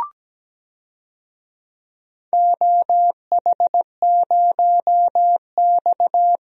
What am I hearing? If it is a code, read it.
Morse code, heard as a keyed tone, about 17 wpm, 700 Hz. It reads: OH0X
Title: DXCC-100 Call Signs - 4 Characters - Repeated 3x - 17wpm